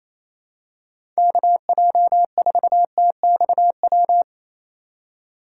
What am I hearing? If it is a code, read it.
Morse code, heard as a keyed tone, about 28 wpm, 700 Hz. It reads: KJ4TXW
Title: USA Call Signs - 6 Characters - 3x Character Spacing - 28wpm